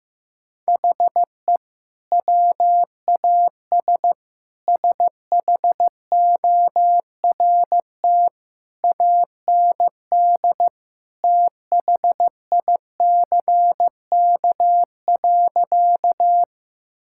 Morse code, heard as a keyed tone, about 15 wpm, 700 Hz. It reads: HE WAS SHORT AND THICK.
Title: Sentences from Top 1600 Words 15wpm